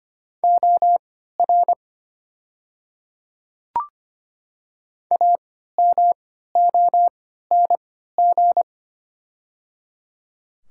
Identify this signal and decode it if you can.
Morse code, heard as a keyed tone, about 25 wpm, spaced as for 14 wpm, 700 Hz. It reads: OR AMONG
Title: 3x Character Spacing - Top 500 Words - 25wpm